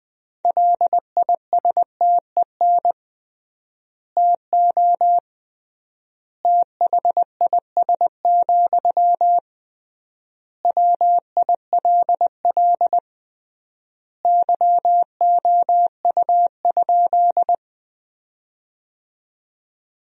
Morse code, heard as a keyed tone, about 20 wpm, 700 Hz. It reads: LISTEN TO THIS, WILL YOU?
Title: 3x Word-Spacing - Sentences from Top 400 Words - 20wpm